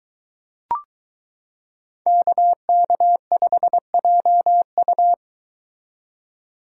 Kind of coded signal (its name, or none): Morse code